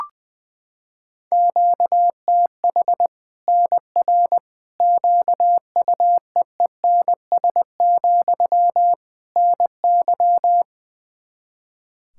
Morse code, heard as a keyed tone, about 21 wpm, 700 Hz. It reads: QTH NR QUEENS, NY
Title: Practice QSO - Exchange 1 - Beginning - 20wpm